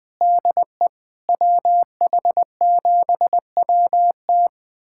Morse code, heard as a keyed tone, about 20 wpm, 700 Hz. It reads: DE WH7WT
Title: USA Callsign DE Callsign - 5 Characters - 20wpm